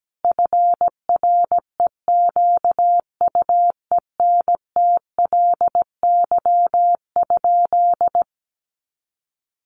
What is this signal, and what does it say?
Morse code, heard as a keyed tone, about 17 wpm, 700 Hz. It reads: FREQUENTLY?
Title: Sentences from Top 1300 Words 17wpm